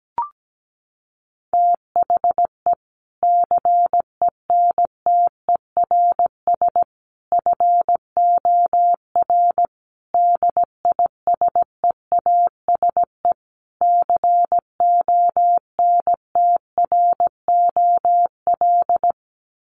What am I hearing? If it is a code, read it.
Morse code, heard as a keyed tone, about 17 wpm, 700 Hz. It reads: THE CENTERS FOR DISEASE CONTROL
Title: Sets of 5 Words Encore2 17wpm